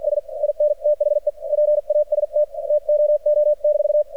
Morse code, heard as a keyed tone, about 53 wpm, spaced as for 46 wpm, 600 Hz, with radio noise. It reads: S IN THE PAST TOO-
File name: archived / CRSnet